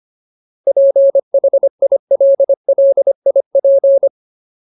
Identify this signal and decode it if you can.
Morse code, heard as a keyed tone, about 25 wpm, 550 Hz. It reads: PHILLIP
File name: Ultimate Word List / 25WPM Ultimate